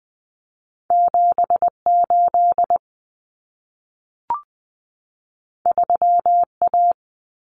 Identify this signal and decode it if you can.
Morse code, heard as a keyed tone, about 20 wpm, 700 Hz. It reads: 78 3A
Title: Common Cut Numbers - 2-Digit 20wpm